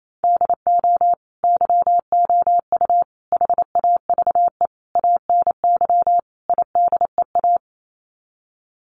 Morse code, heard as a keyed tone, about 28 wpm, 700 Hz. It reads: DO YOU HAVE ANY IDEA
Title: Sets of 5 Words Encore 28wpm